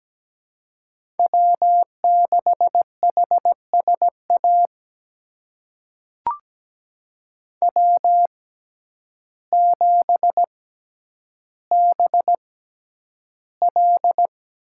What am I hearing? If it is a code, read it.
Morse code, heard as a keyed tone, about 17 wpm, spaced as for 9 wpm, 700 Hz. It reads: W6HSA W7BL
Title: USA Call Signs - 5 Characters - 6x Character Spacing - 17wpm